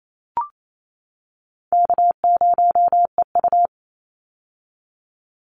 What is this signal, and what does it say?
Morse code, heard as a keyed tone, about 28 wpm, 700 Hz. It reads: K0EU
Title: Easy Call Signs - 28wpm